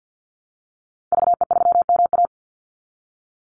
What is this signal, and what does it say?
Morse code, heard as a keyed tone, about 50 wpm, 700 Hz. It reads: VE3MA
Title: Call Signs Repeated 3x - 50wpm